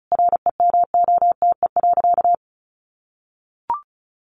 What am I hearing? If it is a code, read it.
Morse code, heard as a keyed tone, about 34 wpm, 700 Hz. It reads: REMOTE.
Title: Sentences from Top 1300 Words 35wpm